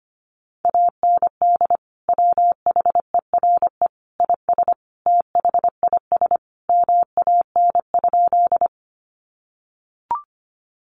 Morse code, heard as a keyed tone, about 25 wpm, 700 Hz. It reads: AND WHERE IS THIS MAN?
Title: Sentences from Top 200 Words 25wpm